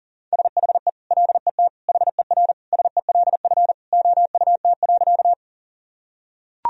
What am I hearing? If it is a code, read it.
Morse code, heard as a keyed tone, about 40 wpm, 700 Hz. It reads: SHE LET HER SELF OUT.
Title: Sentences from Top 1000 Words - Review Part 1 - 40wpm